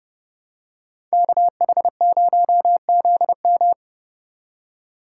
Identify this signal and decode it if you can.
Morse code, heard as a keyed tone, about 30 wpm, 700 Hz. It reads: KH0ZM